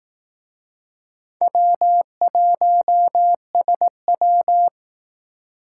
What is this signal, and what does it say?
Morse code, heard as a keyed tone, about 19 wpm, 700 Hz. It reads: W1SW